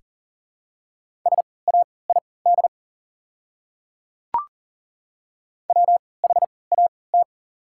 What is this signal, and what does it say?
Morse code, heard as a keyed tone, about 40 wpm, spaced as for 23 wpm, 700 Hz. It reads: SAID WHAT
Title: 3x Character Spacing - Top 100 Words - 40wpm